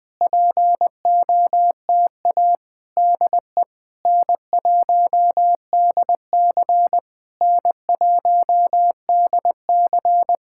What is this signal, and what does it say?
Morse code, heard as a keyed tone, about 20 wpm, 700 Hz. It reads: POTA DE N1DC N1DC
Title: Speed-Racing Common QSO Phrases Encore - 20wpm